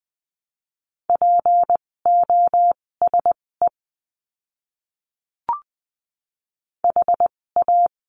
Morse code, heard as a keyed tone, about 20 wpm, spaced as for 16 wpm, 700 Hz. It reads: POSE HA
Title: 256 Lesson 42a - Words 12-20wpm